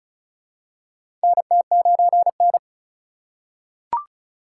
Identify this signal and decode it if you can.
Morse code, heard as a keyed tone, about 34 wpm, 700 Hz. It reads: NT9N
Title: USA Call Signs - 4 Characters - 4x Character Spacing - 35wpm